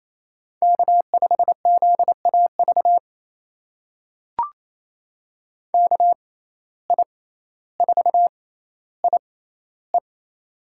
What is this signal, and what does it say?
Morse code, heard as a keyed tone, about 28 wpm, spaced as for 14 wpm, 700 Hz. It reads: K5ZAV KI4IE
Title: USA Call Signs - 5 Characters - 6x Character Spacing - 28wpm